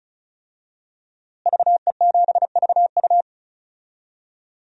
Morse code, heard as a keyed tone, about 36 wpm, 700 Hz. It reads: VE7VU
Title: DXCC-100 Call Signs - 5 Characters - Repeated 3x - 35wpm